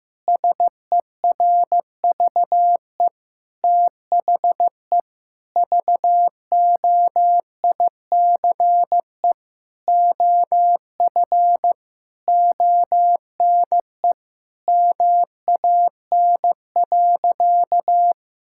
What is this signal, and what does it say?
Morse code, heard as a keyed tone, about 15 wpm, 700 Hz. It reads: SERVE THE VOICE OF ONE MAN.
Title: Sentences from Top 1000 Words - Review Part 6 - 15wpm